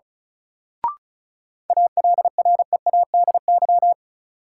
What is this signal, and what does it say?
Morse code, heard as a keyed tone, about 36 wpm, 700 Hz. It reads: ALREADY
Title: Sentences from Top 700 Words - New Word Drill - 35wpm